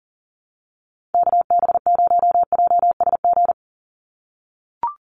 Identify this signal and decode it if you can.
Morse code, heard as a keyed tone, about 40 wpm, 700 Hz. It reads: KB0JSG
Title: USA Call Signs 40wpm